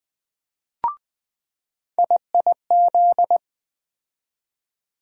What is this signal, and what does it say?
Morse code, heard as a keyed tone, about 20 wpm, 700 Hz. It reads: IIZ
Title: USA Call Sign Postfixes - 3 Letters 20wpm